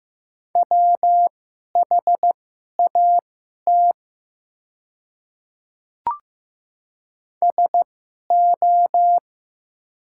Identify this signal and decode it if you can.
Morse code, heard as a keyed tone, about 15 wpm, spaced as for 11 wpm, 700 Hz. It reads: WHAT SO
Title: 2x Character Spacing - Top 300 Words - 15wpm